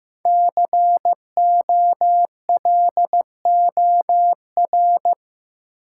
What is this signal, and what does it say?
Morse code, heard as a keyed tone, about 15 wpm, 700 Hz. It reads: COLOR
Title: Sentences from Top 1000 Words - Review Part 2 - New Word Drill - 15wpm